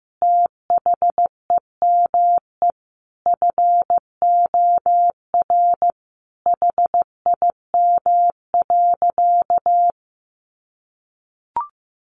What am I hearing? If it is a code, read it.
Morse code, heard as a keyed tone, about 15 wpm, 700 Hz. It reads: THEME FOR HIM.
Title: Sentences from Top 1100 Words 15wpm